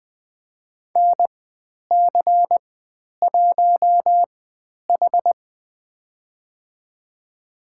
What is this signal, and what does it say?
Morse code, heard as a keyed tone, about 20 wpm, spaced as for 10 wpm, 700 Hz. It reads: NC1H
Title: 102 Lesson 17g - Letter u - Call Signs 20wpm